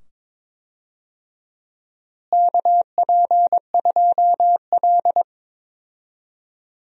Morse code, heard as a keyed tone, about 22 wpm, 700 Hz. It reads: KP2L
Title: DXCC-100 Call Signs - 4 Characters - Repeated 2x - 22wpm